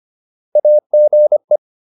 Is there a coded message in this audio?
Yes: Morse code from 0.6 to 1.6 s, heard as a keyed tone, about 25 wpm, 600 Hz. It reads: AGE